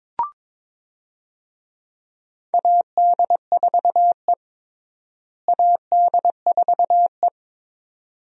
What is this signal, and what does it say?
Morse code, heard as a keyed tone, about 21 wpm, 700 Hz. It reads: AD4E AD4E
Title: USA Call Signs - 4 Characters - Repeated 2x - 22wpm